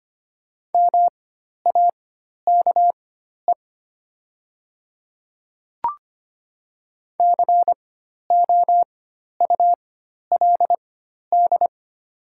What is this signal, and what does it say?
Morse code, heard as a keyed tone, about 25 wpm, spaced as for 12 wpm, 690 Hz. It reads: MAKE COULD
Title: 4x Character Spacing - Top 100 Words - 25wpm